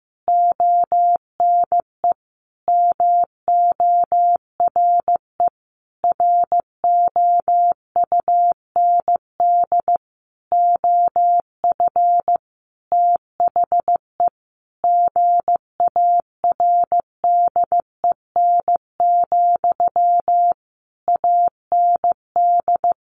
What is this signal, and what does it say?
Morse code, heard as a keyed tone, about 15 wpm, 700 Hz. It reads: ONE MORE ROUND OF THE GARDEN, AND
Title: Sentences from Top 700 Words 15wpm